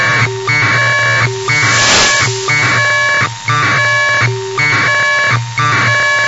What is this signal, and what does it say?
Packet radio (AX.25, as in APRS).
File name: Sohla-1 1200bd packet HiMode by ja0caw